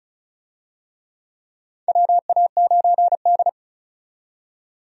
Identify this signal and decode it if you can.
Morse code, heard as a keyed tone, about 36 wpm, 700 Hz. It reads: WA9D
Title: USA Call Signs - 4 Characters - 35wpm